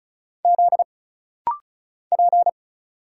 Morse code, heard as a keyed tone, about 36 wpm, 700 Hz. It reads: Z P
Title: Single Letter-Number Alt - Rapid-Fire 35wpm